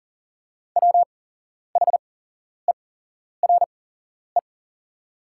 Morse code, heard as a keyed tone, about 40 wpm, spaced as for 11 wpm, 690 Hz. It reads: WHERE
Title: 8x Character Spacing - Top 300 Words - 40wpm